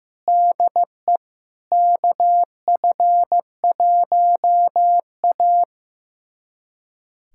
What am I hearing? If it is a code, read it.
Morse code, heard as a keyed tone, about 15 wpm, 700 Hz. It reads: DE KF1A
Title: USA Callsign DE Callsign - 4 Characters - 15wpm